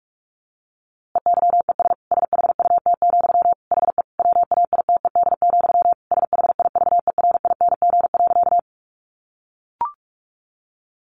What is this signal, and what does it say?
Morse code, heard as a keyed tone, about 45 wpm, 700 Hz. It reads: EYES SHUT, HE WAITED, SHIVERING.
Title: A Sound Of Thunder 45wpm